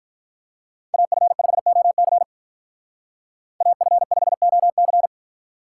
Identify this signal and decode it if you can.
Morse code, heard as a keyed tone, about 44 wpm, 700 Hz. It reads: AF5OC AF5OC
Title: Speed Racing - USA 2x2 Call Signs 35wpm